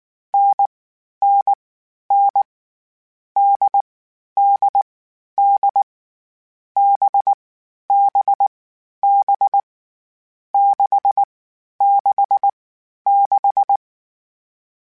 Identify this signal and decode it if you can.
Morse code, heard as a keyed tone, about 19 wpm, 800 Hz. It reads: N N N D D D B B B 6 6 6
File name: Variable Speed